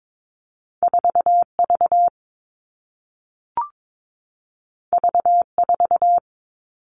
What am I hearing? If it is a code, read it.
Morse code, heard as a keyed tone, about 22 wpm, 700 Hz. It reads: 4V V4
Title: All Cut Numbers - 2-Digit 22wpm